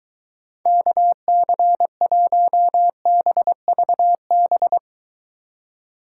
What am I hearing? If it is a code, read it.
Morse code, heard as a keyed tone, about 23 wpm, 700 Hz. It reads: KC1BVB